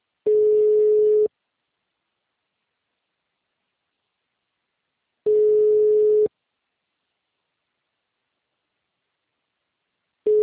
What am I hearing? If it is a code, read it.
Ringback tone (Europe, 425 Hz).